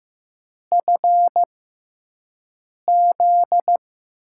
Morse code, heard as a keyed tone, about 15 wpm, spaced as for 5 wpm, 700 Hz. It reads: FZ
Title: USA Call Signs - 6x Character Spacing - 15wpm